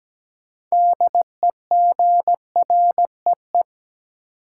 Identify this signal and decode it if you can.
Morse code, heard as a keyed tone, about 17 wpm, 700 Hz. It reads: DEGREE
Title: Sentences from Top 1500 Words - New Word Drill Alt - 17wpm